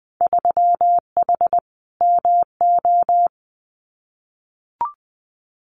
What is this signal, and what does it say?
Morse code, heard as a keyed tone, about 20 wpm, 700 Hz. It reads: 3H MO
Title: Winter Field Day Exchanges Encore 20wpm